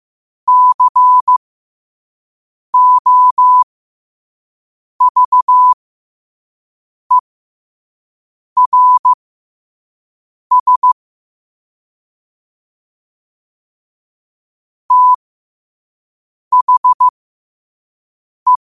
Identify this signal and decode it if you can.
Morse code, heard as a keyed tone, about 15 wpm, spaced as for 5 wpm, 1000 Hz. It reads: COVERS THE
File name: qotdfiles / 5wpm